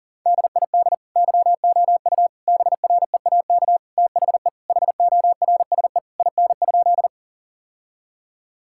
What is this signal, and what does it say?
Morse code, heard as a keyed tone, about 40 wpm, 700 Hz. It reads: DID YOU BREAK THE HORSE IN?
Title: Sentences from Top 1000 Words - Review Part 6 - 40wpm